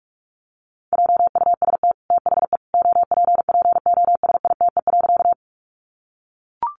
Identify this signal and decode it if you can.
Morse code, heard as a keyed tone, about 44 wpm, 700 Hz. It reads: JUST THE OPPOSITE.
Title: Sentences from Top 1300 Words 45wpm